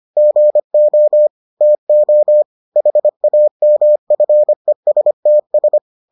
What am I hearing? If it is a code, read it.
Morse code, heard as a keyed tone, about 25 wpm, 600 Hz. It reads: GO TO HAMFESTS